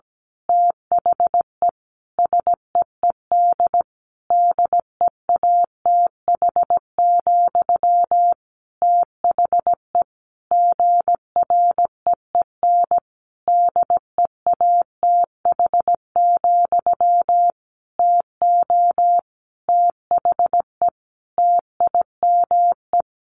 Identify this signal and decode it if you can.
Morse code, heard as a keyed tone, about 17 wpm, 700 Hz. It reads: THE SEED DEATH, THE GREEN DEATH, TO THE TIME
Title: A Sound Of Thunder 17wpm